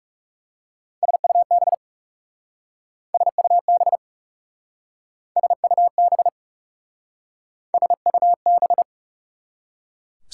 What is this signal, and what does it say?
Morse code, heard as a keyed tone, about 39 wpm, 700 Hz. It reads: SUB SUB SUB SUB